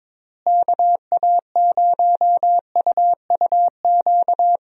Morse code, heard as a keyed tone, about 22 wpm, 700 Hz. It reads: KA0UUQ
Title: USA Call Signs - 6 Characters - Repeated 3x - 22wpm